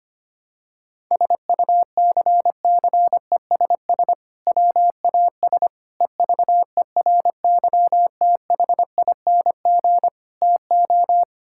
Morse code, heard as a keyed tone, about 25 wpm, 700 Hz. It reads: SUCCESS WAS EVERYTHING TO
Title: Sentences from Top 1000 Words - Review Part 6 - 25wpm